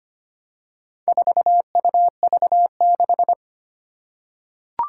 Morse code, heard as a keyed tone, about 25 wpm, 700 Hz. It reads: 4UV6
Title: Four Challenging Letter-Number Combinations 25wpm